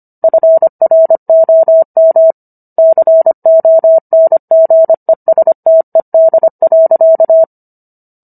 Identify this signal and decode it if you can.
Morse code, heard as a keyed tone, about 25 wpm, 650 Hz. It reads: FROM CONGESTED.